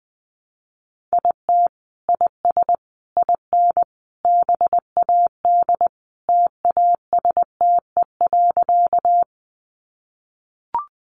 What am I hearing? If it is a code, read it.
Morse code, heard as a keyed tone, about 20 wpm, 700 Hz. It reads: IT IS IN BAD TASTE.
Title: Sentences from Top 1100 Words 20wpm